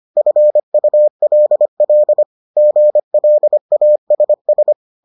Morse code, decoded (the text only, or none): FULL GLASS